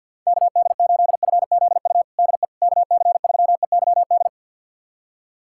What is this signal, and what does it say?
Morse code, heard as a keyed tone, about 52 wpm, 690 Hz. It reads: KD8FZU DE KC3EYD
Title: USA Callsign DE Callsign 50wpm